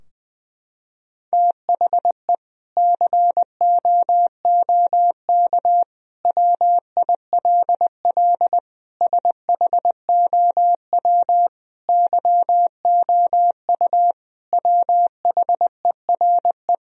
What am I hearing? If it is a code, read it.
Morse code, heard as a keyed tone, about 20 wpm, 700 Hz. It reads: THE COOK WILL SHOW YOU WHERE
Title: Sentences from Top 1000 Words - Review Part 4 - 20wpm